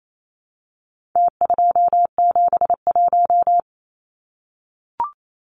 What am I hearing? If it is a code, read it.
Morse code, heard as a keyed tone, about 28 wpm, 700 Hz. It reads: T271
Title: Numbers - Contest Style - 4-Digit 28wpm